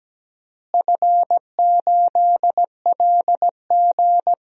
Morse code, heard as a keyed tone, about 17 wpm, 700 Hz. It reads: F8LG
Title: DXCC-100 Call Signs - 4 Characters - Repeated 2x - 17wpm